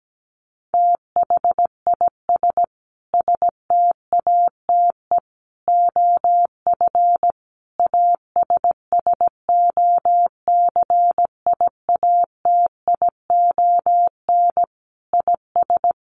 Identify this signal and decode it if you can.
Morse code, heard as a keyed tone, about 17 wpm, 700 Hz. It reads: THIS STATE OF ASSOCIATION IS
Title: Sentences from Top 1100 Words 17wpm